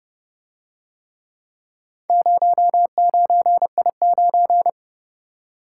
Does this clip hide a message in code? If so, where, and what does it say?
Yes: Morse code from 2.1 to 4.7 s, heard as a keyed tone, about 29 wpm, 700 Hz. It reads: 09I9